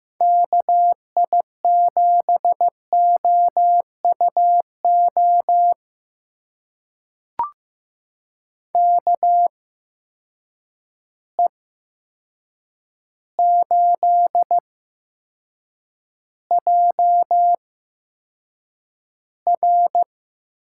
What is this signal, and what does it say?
Morse code, heard as a keyed tone, about 15 wpm, spaced as for 7 wpm, 700 Hz. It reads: KI7OUO KE8JR
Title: USA Call Signs - 6 Characters - 8x Character Spacing - 15wpm